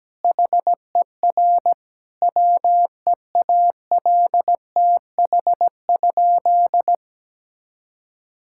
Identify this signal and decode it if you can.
Morse code, heard as a keyed tone, about 17 wpm, 700 Hz. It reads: HER WEALTH?
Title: Sentences from Top 1600 Words 17wpm